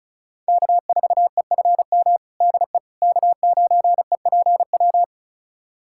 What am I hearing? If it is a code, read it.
Morse code, heard as a keyed tone, about 36 wpm, 700 Hz. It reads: K4EFM DE K9EPW
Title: USA Callsign DE Callsign - 5 Characters - 35wpm